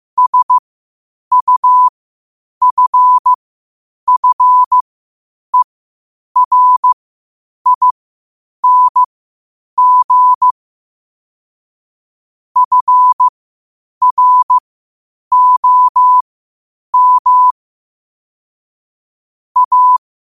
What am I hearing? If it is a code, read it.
Morse code, heard as a keyed tone, about 15 wpm, spaced as for 8 wpm, 1000 Hz. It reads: SUFFERING FROM A
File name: qotdfiles / 8wpm